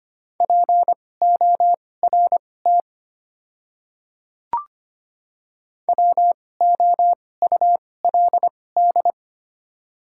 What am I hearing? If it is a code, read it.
Morse code, heard as a keyed tone, about 25 wpm, spaced as for 18 wpm, 700 Hz. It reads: PORT WOULD
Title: 2x Character Spacing - Top 200 Words - 25wpm